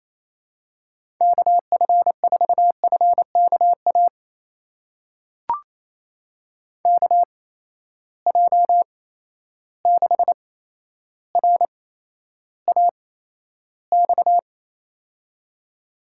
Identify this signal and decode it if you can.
Morse code, heard as a keyed tone, about 28 wpm, spaced as for 12 wpm, 700 Hz. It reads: KF4FKA KJ6RAX
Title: USA Call Signs - 6 Characters - 8x Character Spacing - 28wpm